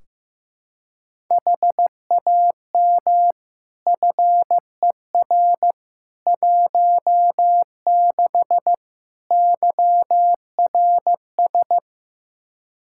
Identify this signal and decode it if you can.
Morse code, heard as a keyed tone, about 15 wpm, 700 Hz. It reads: HAM FER 16 YRS
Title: Practice QSO - Exchange 3 - More Info - 15wpm